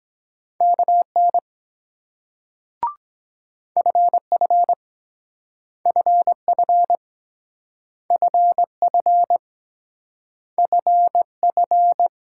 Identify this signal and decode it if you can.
Morse code, heard as a keyed tone, about 23 wpm, 700 Hz. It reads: KN FF FF FF FF